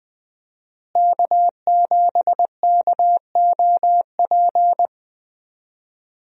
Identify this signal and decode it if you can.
Morse code, heard as a keyed tone, about 20 wpm, 700 Hz. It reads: K7KOP
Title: USA Call Signs - 5 Characters - Repeated 2x - 20wpm